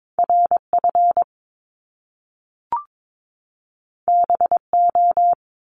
Morse code, heard as a keyed tone, about 22 wpm, 700 Hz. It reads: RF BO
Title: Top Two Letter Combinations in Words 22wpm